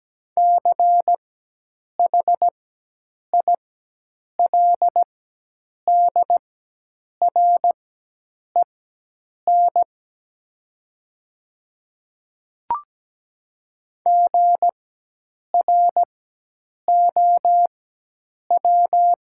Morse code, heard as a keyed tone, about 17 wpm, spaced as for 8 wpm, 700 Hz. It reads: CHILDREN GROW
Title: 4x Character Spacing - Top 500 Words - 17wpm